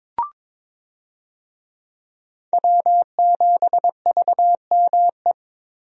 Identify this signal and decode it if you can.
Morse code, heard as a keyed tone, about 22 wpm, 700 Hz. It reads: W7VME
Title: USA Call Signs - 5 Characters - Repeated 3x - 22wpm